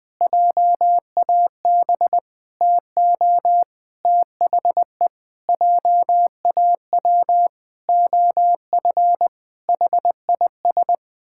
Morse code, heard as a keyed tone, about 20 wpm, 700 Hz. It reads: JAB TO THE JAW OF HIS
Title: Pangram Challenge v2 - 20wpm